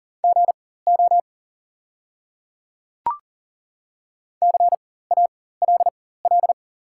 Morse code, heard as a keyed tone, about 39 wpm, spaced as for 19 wpm, 690 Hz. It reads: GO CALL
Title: 4x Character Spacing - Top 100 Words - 40wpm